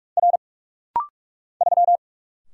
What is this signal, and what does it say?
Morse code, heard as a keyed tone, about 45 wpm, 700 Hz. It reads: R 3
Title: Single Letter-Number Alt - Rapid-Fire 45wpm